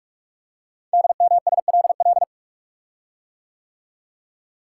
Morse code, heard as a keyed tone, about 44 wpm, 700 Hz. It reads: DMSLL
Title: Code Groups - 5 Letter:Numbers - 45wpm